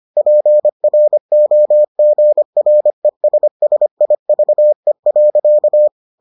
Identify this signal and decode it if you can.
Morse code, heard as a keyed tone, about 25 wpm, 600 Hz. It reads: PROGRESSIVE.